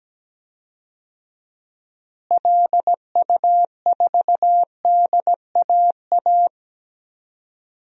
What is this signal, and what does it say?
Morse code, heard as a keyed tone, about 17 wpm, 700 Hz. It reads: LU4DAA
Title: DXCC-100 Call Signs - 6 Characters - Repeated 3x - 17wpm